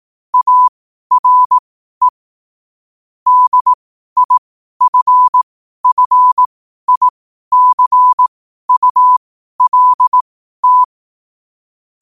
Morse code, heard as a keyed tone, about 18 wpm, spaced as for 12 wpm, 1000 Hz. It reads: ARE DIFFICULT
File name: qotdfiles / 12wpm